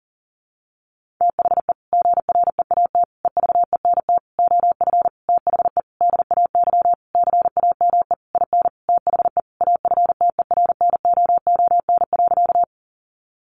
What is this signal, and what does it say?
Morse code, heard as a keyed tone, about 40 wpm, 700 Hz. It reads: THE GREAT EVENT OF THE DAY CAME IN THE AFTERNOON.
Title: Sentences from Top 1500 Words 40wpm